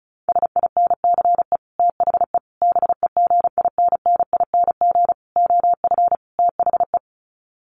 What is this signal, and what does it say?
Morse code, heard as a keyed tone, about 34 wpm, 700 Hz. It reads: SINCE THE BEGINNING OF THE
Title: Sets of 5 Words Encore2 35wpm